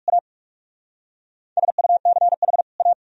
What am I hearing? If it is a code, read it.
Morse code, heard as a keyed tone, about 57 wpm, 690 Hz. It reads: A SUCH A